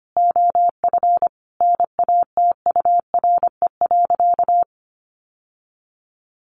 Morse code, heard as a keyed tone, about 25 wpm, 700 Hz. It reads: OF NATURE.
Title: Sentences from Top 1200 Words 25wpm